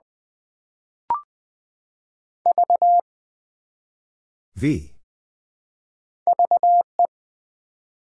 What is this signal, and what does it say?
Morse code, heard as a keyed tone, about 20 wpm, 700 Hz. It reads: V VE
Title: Build A Word - 1 Letter At A Time - 4-Letter Words - 20wpm